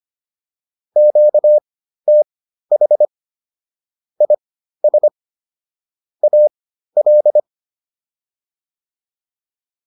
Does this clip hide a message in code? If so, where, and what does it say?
Yes: Morse code from 1.0 to 7.4 s, heard as a keyed tone, about 25 wpm, spaced as for 13 wpm, 600 Hz. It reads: QTH IS AL